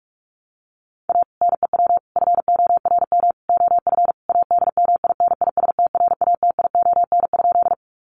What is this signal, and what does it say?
Morse code, heard as a keyed tone, about 44 wpm, 700 Hz. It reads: A NEW FORM OF ADMINISTRATION?
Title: Sentences from Top 1300 Words 45wpm